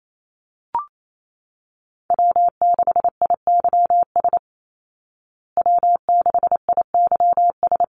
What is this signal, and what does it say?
Morse code, heard as a keyed tone, about 28 wpm, 700 Hz. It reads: W6IYS W6IYS
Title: Call Signs Repeated 3x - 28wpm